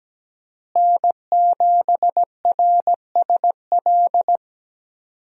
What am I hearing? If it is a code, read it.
Morse code, heard as a keyed tone, about 17 wpm, 700 Hz. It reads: N7RSL
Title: USA Call Signs - 5 Characters - Repeated 3x - 17wpm